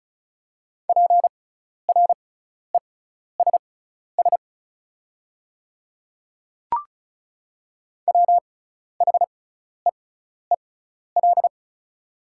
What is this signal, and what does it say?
Morse code, heard as a keyed tone, about 36 wpm, spaced as for 12 wpm, 700 Hz. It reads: PRESS WHEEL
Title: 6x Character Spacing - Top 500 Words - 35wpm